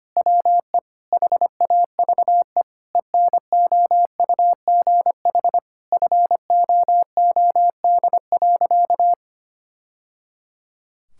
Morse code, heard as a keyed tone, about 25 wpm, 700 Hz. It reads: WE HAVE ENOUGH FOOD.
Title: Sentences from Top 1000 Words - Review Part 5 - 25wpm